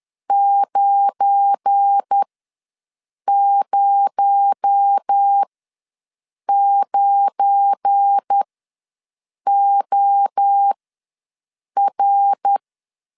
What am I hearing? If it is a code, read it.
Morse code, heard as a keyed tone, about 10 wpm, spaced as for 6 wpm, 800 Hz. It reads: 909OR
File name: LEZIONI / 7 GF9Z / 7 GF9 ws 10wpm 4034